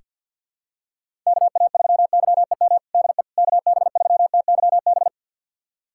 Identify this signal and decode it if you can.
Morse code, heard as a keyed tone, about 47 wpm, 700 Hz. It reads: KA3YEM DE KB3TYB
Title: USA Callsign DE Callsign 50wpm